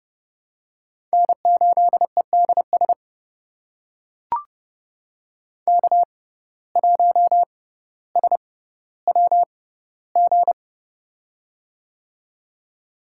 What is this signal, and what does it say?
Morse code, heard as a keyed tone, about 30 wpm, spaced as for 15 wpm, 700 Hz. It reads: N8EDS K1SWG
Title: USA Call Signs - 5 Characters - 6x Character Spacing - 30wpm